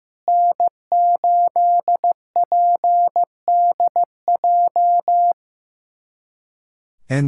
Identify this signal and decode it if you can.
Morse code, heard as a keyed tone, about 15 wpm, 700 Hz. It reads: N8PDJ
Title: USA Callsign DE Callsign - 5 Characters - 15wpm